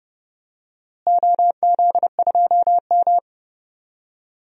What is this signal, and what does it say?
Morse code, heard as a keyed tone, about 30 wpm, 700 Hz. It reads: OZ2M
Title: DXCC-100 Call Signs - 4 Characters - Repeated 2x - 30wpm